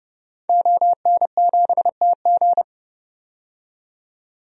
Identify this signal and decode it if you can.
Morse code, heard as a keyed tone, about 32 wpm, 700 Hz. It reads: ON7TG
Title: DXCC-100 Call Signs - 5 Characters - 30wpm